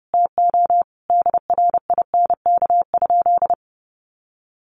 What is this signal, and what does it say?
Morse code, heard as a keyed tone, about 30 wpm, 700 Hz. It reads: TO DRINK?
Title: Sentences from Top 1500 Words 30wpm